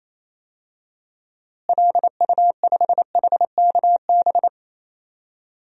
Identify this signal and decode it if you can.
Morse code, heard as a keyed tone, about 27 wpm, 700 Hz. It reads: LU5HKB
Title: DXCC-100 Call Signs - 6 Characters - Repeated 3x - 28wpm